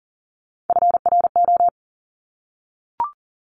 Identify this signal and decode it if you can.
Morse code, heard as a keyed tone, about 40 wpm, 690 Hz. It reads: FRO
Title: Top Three Letter Combinations in Words 40wpm